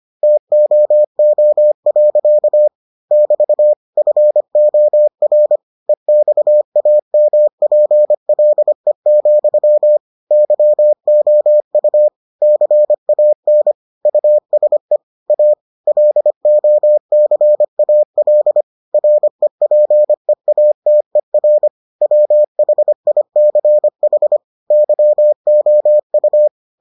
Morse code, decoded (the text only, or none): TOO. <BT> FOR EXAMPLE, YOU CAN USE A LOCAL REPEATER WHICH YOU